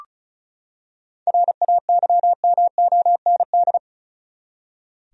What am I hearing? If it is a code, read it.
Morse code, heard as a keyed tone, about 36 wpm, 700 Hz. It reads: RAYMOND
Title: USA Common Names - 35wpm